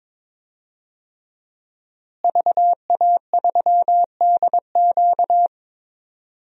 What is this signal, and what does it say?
Morse code, heard as a keyed tone, about 22 wpm, 700 Hz. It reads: VA3DQ
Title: DXCC-100 Call Signs - 5 Characters - Repeated 3x - 22wpm